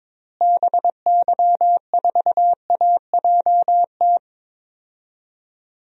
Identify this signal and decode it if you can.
Morse code, heard as a keyed tone, about 22 wpm, 700 Hz. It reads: BY4AJT
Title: DXCC-100 Call Signs - 6 Characters - Repeated 2x - 22wpm